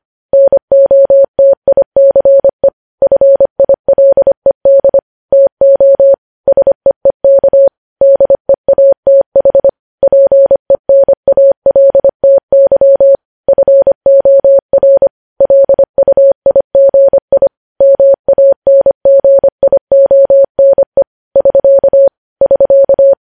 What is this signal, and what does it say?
Morse code, heard as a keyed tone, about 25 wpm, 550 Hz. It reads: NOTICE FILED TO SEEK DEATH PENALTY FOR LUIGI MANGIONE <SK> <SK>